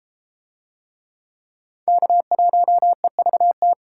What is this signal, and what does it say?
Morse code, heard as a keyed tone, about 32 wpm, 700 Hz. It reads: K1EVT